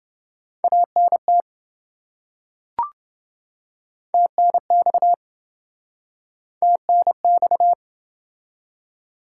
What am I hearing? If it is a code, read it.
Morse code, heard as a keyed tone, about 25 wpm, 700 Hz. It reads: ANT TNX TNX
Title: Speed-Racing Top 20 QSO Elements - 20wpm